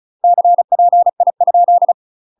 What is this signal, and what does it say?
Morse code, heard as a keyed tone, about 36 wpm, 700 Hz. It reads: CPI ?
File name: UK MASTER QSOs TRAINING / 35WPM / UK MASTER QSOs 0011